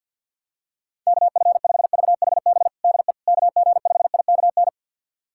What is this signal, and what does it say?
Morse code, heard as a keyed tone, about 51 wpm, 700 Hz. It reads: KF5VHB DE KG5IKN
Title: USA Callsign DE Callsign - 6 Characters - 50wpm